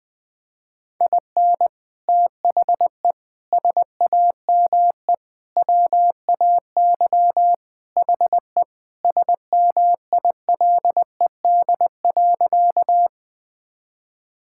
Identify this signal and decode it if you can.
Morse code, heard as a keyed tone, about 20 wpm, 700 Hz. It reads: IN THE SAME WAY HE SMILED.
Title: Sentences with 4-Word Phrases - 20wpm